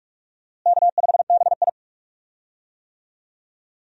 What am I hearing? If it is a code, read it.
Morse code, heard as a keyed tone, about 46 wpm, 700 Hz. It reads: K5BI
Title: USA Call Signs - 4 Characters - Repeated 2x - 45wpm